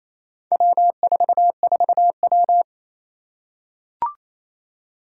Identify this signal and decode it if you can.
Morse code, heard as a keyed tone, about 28 wpm, 700 Hz. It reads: W44W
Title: Four Challenging Letter-Number Combinations 28wpm